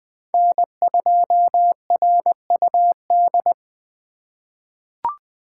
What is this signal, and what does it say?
Morse code, heard as a keyed tone, about 20 wpm, 700 Hz. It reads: N2RUD
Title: USA Call Signs - 5 Characters - 4x Character Spacing - 20wpm